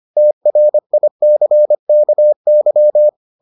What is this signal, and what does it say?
Morse code, heard as a keyed tone, about 25 wpm, 600 Hz. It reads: TRICKY